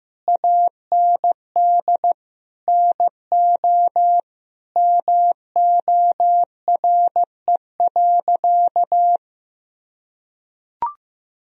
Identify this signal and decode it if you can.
Morse code, heard as a keyed tone, about 15 wpm, 700 Hz. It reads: AND NO MORE.
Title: Sentences from Top 1000 Words - Review Part 1 - 15wpm